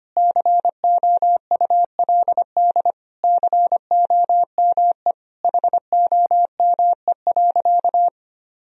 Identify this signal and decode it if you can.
Morse code, heard as a keyed tone, about 25 wpm, 700 Hz. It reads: COULD COME HOME.
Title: Sentences from Top 1000 Words - Review Part 3 - 25wpm